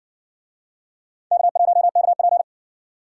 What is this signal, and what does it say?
Morse code, heard as a keyed tone, about 60 wpm, 690 Hz. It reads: K1KP